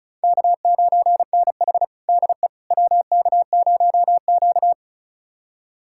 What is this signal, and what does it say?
Morse code, heard as a keyed tone, about 36 wpm, 700 Hz. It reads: K9NH DE WK0Q
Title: USA Callsign DE Callsign - 4 Characters - 35wpm